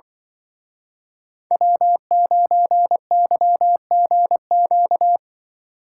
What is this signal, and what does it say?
Morse code, heard as a keyed tone, about 25 wpm, 700 Hz. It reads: W9YGQ
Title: Speed Racing - USA 1x3 Call Signs 20wpm